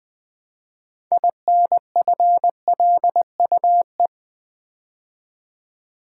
Morse code, heard as a keyed tone, about 20 wpm, 700 Hz. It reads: INFLUE
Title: Build A Word - 3 Letters At A Time - 9-Letter Words - 20wpm